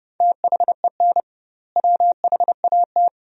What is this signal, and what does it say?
Morse code, heard as a keyed tone, about 30 wpm, 700 Hz. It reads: THEN WHAT
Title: 2x Word-Spacing - Sentences from Top 400 Words - 30wpm